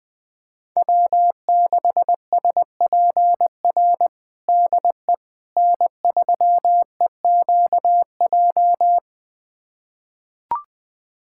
Morse code, heard as a keyed tone, about 20 wpm, 700 Hz. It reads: W6SPR DE N3EQJ
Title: USA Callsign DE Callsign - 5 Characters - 20wpm